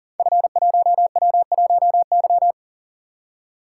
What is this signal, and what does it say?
Morse code, heard as a keyed tone, about 39 wpm, 700 Hz. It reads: F1W1Y
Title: Code Groups - 5 Letter:Numbers - 40wpm